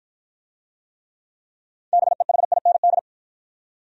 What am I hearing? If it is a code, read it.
Morse code, heard as a keyed tone, about 53 wpm, 700 Hz. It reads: BEHIND